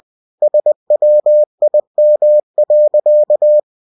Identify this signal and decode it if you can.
Morse code, heard as a keyed tone, about 20 wpm, 600 Hz. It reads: SWIM.